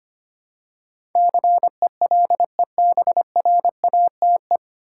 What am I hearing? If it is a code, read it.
Morse code, heard as a keyed tone, about 25 wpm, 700 Hz. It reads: CELEBRATE
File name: Nine Letter Words - 25wpm